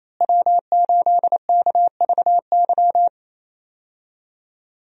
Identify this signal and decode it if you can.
Morse code, heard as a keyed tone, about 28 wpm, 700 Hz. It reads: W8KVY
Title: USA Call Signs - 5 Characters - Repeated 2x - 28wpm